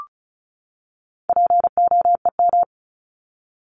Morse code, heard as a keyed tone, about 34 wpm, 700 Hz. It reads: POEM
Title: Top 1751-2000 Words 35wpm